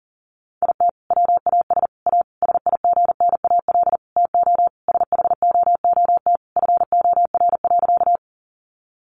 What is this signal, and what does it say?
Morse code, heard as a keyed tone, about 40 wpm, 700 Hz. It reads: IT WAS A SIGNAL TO SHOOT FOR.
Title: Sentences from Top 1400 Words 40wpm